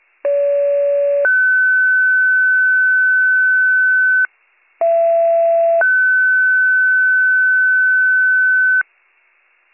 Two-tone page, heard as a one-second tone, then a three-second tone, about 570 then 1500 Hz, about 670 then 1500 Hz. It is heard from 0.2 to 8.8 s.